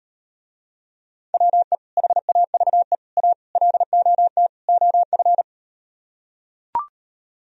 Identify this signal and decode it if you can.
Morse code, heard as a keyed tone, about 38 wpm, 700 Hz. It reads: WE HAVE A LOT OF